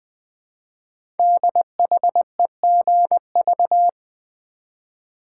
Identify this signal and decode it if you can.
Morse code, heard as a keyed tone, about 20 wpm, 700 Hz. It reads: DHEGV
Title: Code Groups - 5 Letters - 20wpm